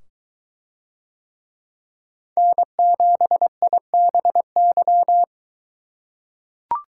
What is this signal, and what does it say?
Morse code, heard as a keyed tone, about 22 wpm, 700 Hz. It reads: N7IBY